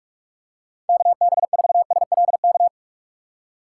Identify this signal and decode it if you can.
Morse code, heard as a keyed tone, about 46 wpm, 700 Hz. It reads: KB4SLK
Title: USA Call Signs - 6 Characters - 4x Character Spacing - 45wpm